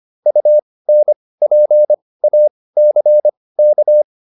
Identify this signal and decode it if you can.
Morse code, heard as a keyed tone, about 25 wpm, spaced as for 18 wpm, 600 Hz. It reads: UNPACK